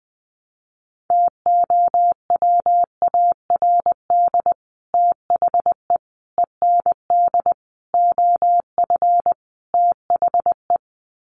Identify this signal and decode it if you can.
Morse code, heard as a keyed tone, about 20 wpm, 700 Hz. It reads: TOWARD THE END OF THE
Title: Sets of 5 Words 20wpm